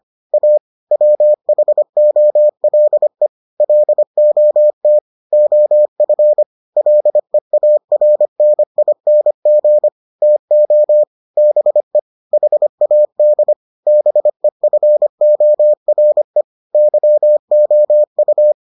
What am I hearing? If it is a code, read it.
Morse code, heard as a keyed tone, about 25 wpm, 600 Hz. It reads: A WHOLE LOT OF LEARNING TO BE HAD BEFORE YOU